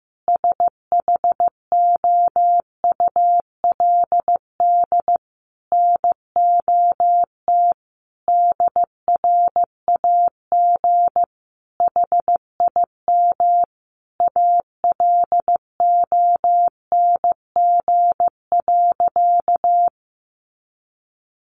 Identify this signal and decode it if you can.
Morse code, heard as a keyed tone, about 15 wpm, 700 Hz. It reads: SHOULD NOT DRAG HIM ALONG.
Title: Sentences from Top 1500 Words 15wpm